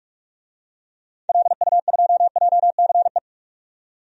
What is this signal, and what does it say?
Morse code, heard as a keyed tone, about 44 wpm, 700 Hz. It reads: LU2JCE
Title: DXCC-100 Call Signs - 6 Characters - 45wpm